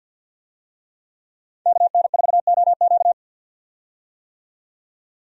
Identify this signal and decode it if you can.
Morse code, heard as a keyed tone, about 45 wpm, 700 Hz. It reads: KN4OQ
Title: USA Call Signs - 5 Characters - 50wpm